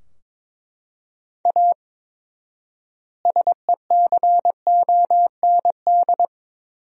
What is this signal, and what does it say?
Morse code, heard as a keyed tone, about 22 wpm, 700 Hz. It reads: A SECOND
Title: 4x Word-Spacing - Sentences from Top 300 Words - 22wpm